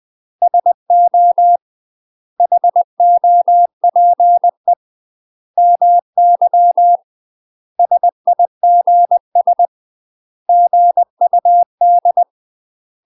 Morse code, heard as a keyed tone, about 20 wpm, 700 Hz. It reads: SO HOPE MY SIGS GUD
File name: 20WPM